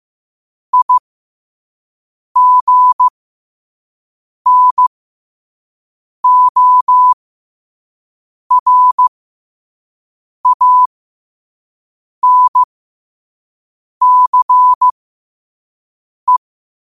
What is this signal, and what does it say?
Morse code, heard as a keyed tone, about 15 wpm, spaced as for 5 wpm, 1000 Hz. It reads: IGNORANCE
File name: qotdfiles / 5wpm